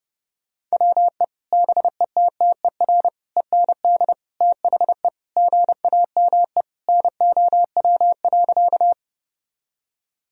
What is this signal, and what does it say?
Morse code, heard as a keyed tone, about 30 wpm, 700 Hz. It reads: WE BETTER END THE GAME NOW.
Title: Sentences from Top 1000 Words - Review Part 3 - 30wpm